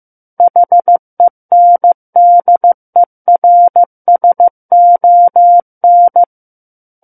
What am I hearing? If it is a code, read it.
Morse code, heard as a keyed tone, about 15 wpm, 700 Hz. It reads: HENDERSON
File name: Book One / 06-Chapter 6 - 15 WPM